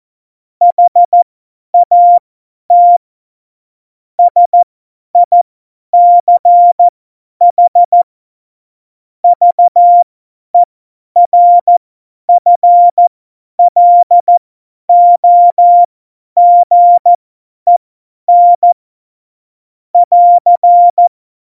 Morse code, heard as a keyed tone, about 14 wpm, spaced as for 10 wpm, 700 Hz. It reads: HAT SICH VERFLOGEN <AR>
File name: Morsekurs / Witze / Witz 24. November 2020, 50 F70